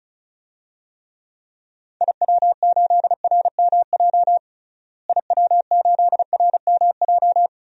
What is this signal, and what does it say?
Morse code, heard as a keyed tone, about 34 wpm, 700 Hz. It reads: IW8RMJ IW8RMJ
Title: DXCC-100 Call Signs - 6 Characters - Repeated 2x - 35wpm